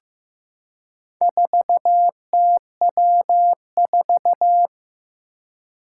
Morse code, heard as a keyed tone, about 15 wpm, 700 Hz. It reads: 4TW4
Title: Four Challenging Letter-Number Combinations 15wpm